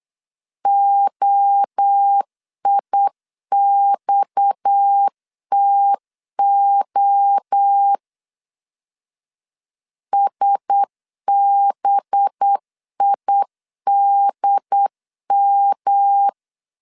Morse code, heard as a keyed tone, about 8 wpm, 800 Hz. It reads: OIXTO SBIDM
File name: CW / DET / 1 6 1500C S B 8wpm 5142